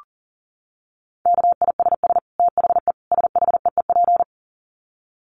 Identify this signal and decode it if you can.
Morse code, heard as a keyed tone, about 40 wpm, 700 Hz. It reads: KISS THE SHEEP
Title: Dit Challenge 40wpm